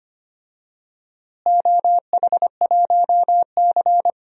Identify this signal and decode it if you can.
Morse code, heard as a keyed tone, about 25 wpm, 700 Hz. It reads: OH1C
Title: DXCC-100 Call Signs - 4 Characters - Repeated 3x - 25wpm